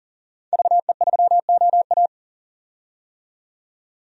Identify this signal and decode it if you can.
Morse code, heard as a keyed tone, about 40 wpm, 700 Hz. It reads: VE3OA
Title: DXCC-100 Call Signs - 5 Characters - 40wpm